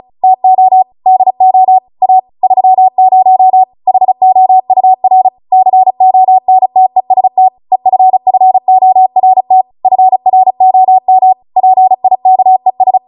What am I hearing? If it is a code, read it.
Morse code, heard as a keyed tone, about 36 wpm, 750 Hz. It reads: TO DO A 30 HOUR CONTEST EFFORT FROM PIKES
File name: Morse / 35WPM